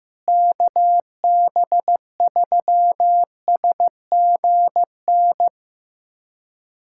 Morse code, heard as a keyed tone, about 15 wpm, 700 Hz. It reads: KB3SGN
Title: USA Callsign DE Callsign 15wpm